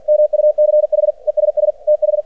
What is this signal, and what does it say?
Morse code, heard as a keyed tone, about 49 wpm, 600 Hz. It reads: MUCH ESS TH
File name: archived / CRSnet